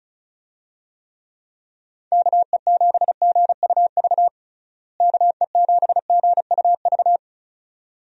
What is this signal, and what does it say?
Morse code, heard as a keyed tone, about 36 wpm, 700 Hz. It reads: KE7GUV KE7GUV
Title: USA Call Signs - 6 Characters - Repeated 2x - 35wpm